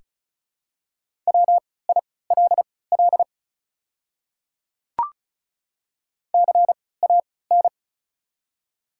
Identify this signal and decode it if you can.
Morse code, heard as a keyed tone, about 36 wpm, spaced as for 20 wpm, 700 Hz. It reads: WILL CAN
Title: 3x Character Spacing - Top 300 Words - 35wpm